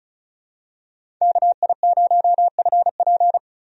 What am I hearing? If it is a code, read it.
Morse code, heard as a keyed tone, about 36 wpm, 700 Hz. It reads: KI0FP
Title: USA Call Signs - 5 Characters - 35wpm